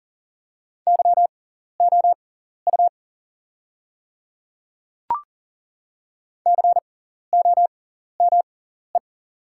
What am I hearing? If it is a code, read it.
Morse code, heard as a keyed tone, about 40 wpm, spaced as for 14 wpm, 700 Hz. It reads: YOU COME
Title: 6x Character Spacing - Top 100 Words - 40wpm